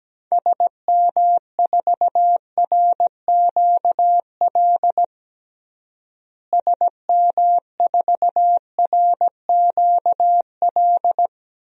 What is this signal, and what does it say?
Morse code, heard as a keyed tone, about 17 wpm, 700 Hz. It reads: SM4RQL SM4RQL
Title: DXCC-100 Call Signs - 6 Characters - Repeated 2x - 17wpm